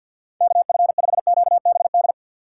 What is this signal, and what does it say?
Morse code, heard as a keyed tone, about 48 wpm, 700 Hz. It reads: KF5QBD
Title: USA Call Signs - 6 Characters - 6x Character Spacing - 50wpm